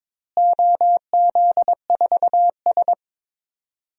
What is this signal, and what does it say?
Morse code, heard as a keyed tone, about 22 wpm, 700 Hz. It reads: OZ4S
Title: DXCC-100 Call Signs - 4 Characters - Repeated 3x - 22wpm